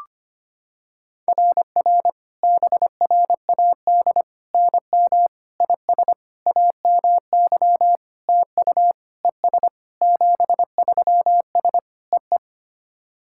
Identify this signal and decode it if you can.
Morse code, heard as a keyed tone, about 25 wpm, 700 Hz. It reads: RR BRAD NM IS AMY TU ES 73S EE
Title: SOTA QSO Practice 25wpm